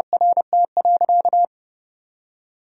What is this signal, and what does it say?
Morse code, heard as a keyed tone, about 31 wpm, 700 Hz. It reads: RT.
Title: Sentences from Top 1000 Words - Review Part 3 - 30wpm